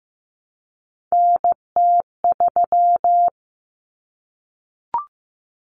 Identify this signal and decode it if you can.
Morse code, heard as a keyed tone, about 15 wpm, 700 Hz. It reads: NT3
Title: Common Cut Numbers - 3-Digit 15wpm